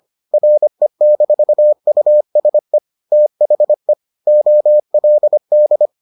Morse code, heard as a keyed tone, about 25 wpm, 600 Hz. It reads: RE-USE THE OLD